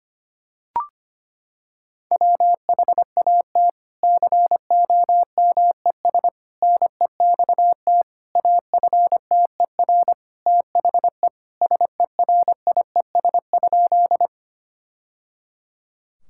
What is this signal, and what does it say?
Morse code, heard as a keyed tone, about 25 wpm, 700 Hz. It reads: WHAT COMES NEXT AFTER THE SERIES?
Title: Sentences from Top 1000 Words - Review Part 5 - 25wpm